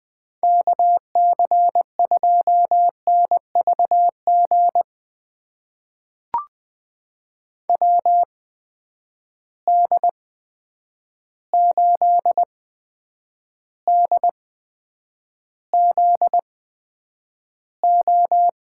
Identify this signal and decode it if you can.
Morse code, heard as a keyed tone, about 20 wpm, spaced as for 9 wpm, 700 Hz. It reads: KC2NVG WD8DZO
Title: USA Call Signs - 6 Characters - 8x Character Spacing - 20wpm